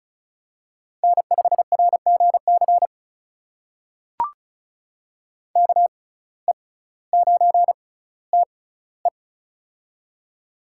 Morse code, heard as a keyed tone, about 36 wpm, spaced as for 18 wpm, 700 Hz. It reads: N5RGC KE9TE
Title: USA Call Signs - 5 Characters - 6x Character Spacing - 35wpm